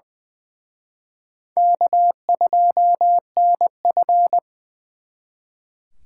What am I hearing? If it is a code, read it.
Morse code, heard as a keyed tone, about 23 wpm, 700 Hz. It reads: K2NF
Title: Speed Racing - USA 1x2 Call Signs 20wpm